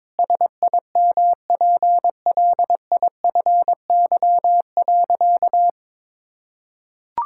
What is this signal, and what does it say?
Morse code, heard as a keyed tone, about 22 wpm, 700 Hz. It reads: SIMPLIFY.
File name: Sentences with 7-Word Phrases - 22wpm